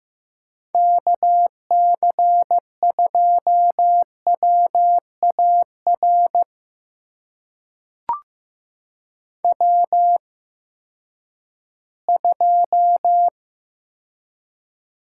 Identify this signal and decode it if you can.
Morse code, heard as a keyed tone, about 15 wpm, spaced as for 10 wpm, 700 Hz. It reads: KC2WAR W2
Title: USA Call Signs - 8x Character Spacing - 15wpm